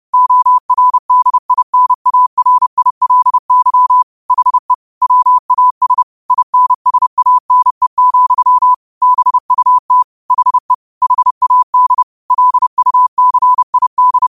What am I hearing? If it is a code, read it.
Morse code, heard as a keyed tone, about 30 wpm, 1000 Hz. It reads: ORDINARILY HE WAS INSANE, BUT HE HAD LUCID
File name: qotdfiles / 30wpm